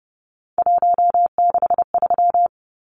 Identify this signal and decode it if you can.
Morse code, heard as a keyed tone, about 30 wpm, 700 Hz. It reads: 163
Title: Numbers - 3-Digit 30wpm